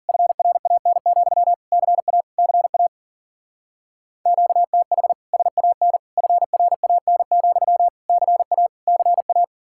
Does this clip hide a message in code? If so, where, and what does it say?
Yes: Morse code from 0.1 to 9.4 s, heard as a keyed tone, about 50 wpm, 700 Hz. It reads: FRAN, CA CA QTH SAN FRAN, CA CA